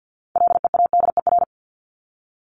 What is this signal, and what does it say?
Morse code, heard as a keyed tone, about 48 wpm, 700 Hz. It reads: LEADER
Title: Top 1501-1750 Words 50wpm